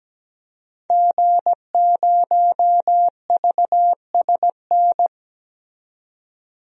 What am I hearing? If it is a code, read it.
Morse code, heard as a keyed tone, about 17 wpm, 700 Hz. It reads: G0VSN
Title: DXCC-100 Call Signs - 5 Characters - 17wpm